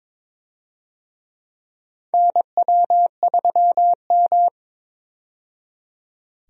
Morse code, heard as a keyed tone, about 22 wpm, 700 Hz. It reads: NW3M
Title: USA Call Signs - 4 Characters - 22wpm